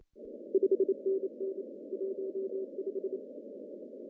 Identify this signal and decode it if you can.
Morse code, heard as a keyed tone, about 27 wpm, 380 Hz. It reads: 5NN 15
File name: CW / cqww-cw-k5tr